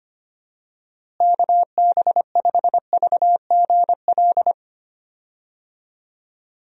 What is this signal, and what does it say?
Morse code, heard as a keyed tone, about 25 wpm, 700 Hz. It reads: KB5VGL
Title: USA Call Signs - 6 Characters - Repeated 3x - 25wpm